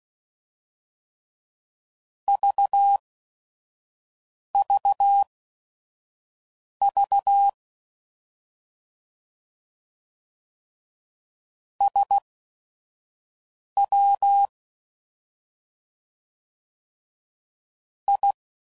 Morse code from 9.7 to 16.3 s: SW

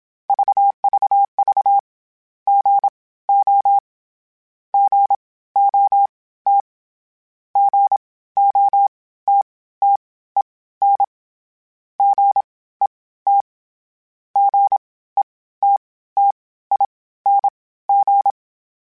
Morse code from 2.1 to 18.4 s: GO GOT GOTTEN GET GETTING

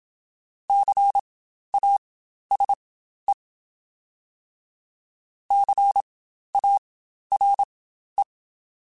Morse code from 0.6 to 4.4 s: CASE

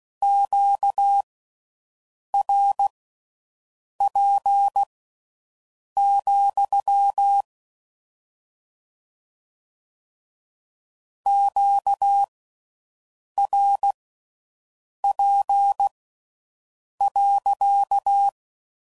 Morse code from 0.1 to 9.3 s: QRP,

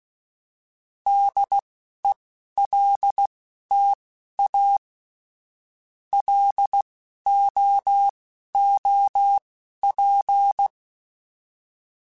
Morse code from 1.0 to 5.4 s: DELTA

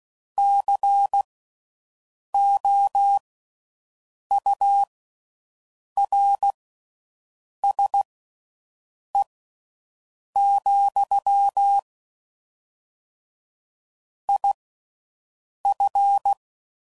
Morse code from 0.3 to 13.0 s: COURSE,

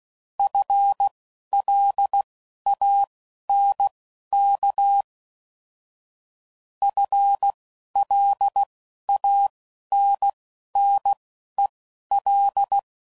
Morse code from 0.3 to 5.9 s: FLANK

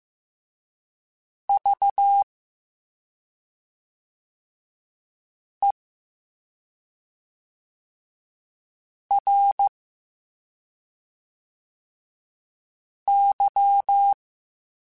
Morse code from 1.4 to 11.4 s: V E R

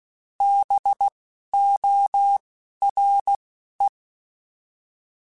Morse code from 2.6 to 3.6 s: R

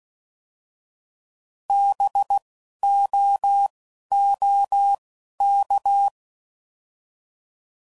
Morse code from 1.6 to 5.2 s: B O O

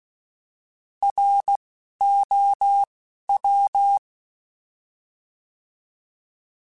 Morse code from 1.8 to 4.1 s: O W

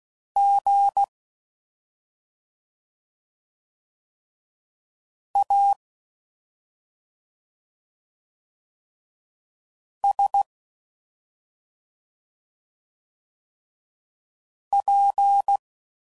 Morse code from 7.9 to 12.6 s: S

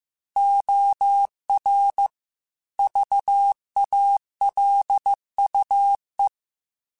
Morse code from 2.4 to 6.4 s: VALUE

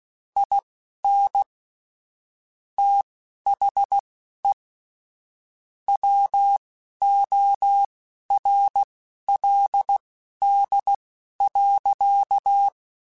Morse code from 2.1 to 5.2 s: THE

